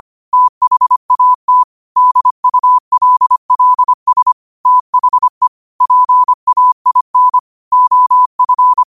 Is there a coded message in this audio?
Yes: Morse code from 0.3 to 8.8 s, heard as a keyed tone, about 25 wpm, 1000 Hz. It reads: THAT DULLS THE PAIN OF